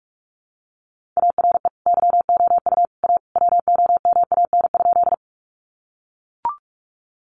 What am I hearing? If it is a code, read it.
Morse code, heard as a keyed tone, about 44 wpm, 700 Hz. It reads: ARE YOU A WOMAN?
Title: Sentences from Top 1100 Words 45wpm